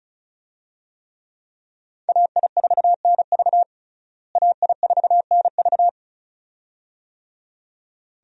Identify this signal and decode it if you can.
Morse code, heard as a keyed tone, about 36 wpm, 700 Hz. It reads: AI4NV AI4NV
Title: USA Call Signs - 5 Characters - Repeated 2x - 35wpm